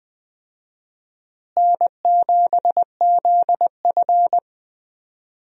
Morse code, heard as a keyed tone, about 21 wpm, 700 Hz. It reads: N7ZF
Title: USA Call Signs - 4 Characters - Repeated 2x - 20wpm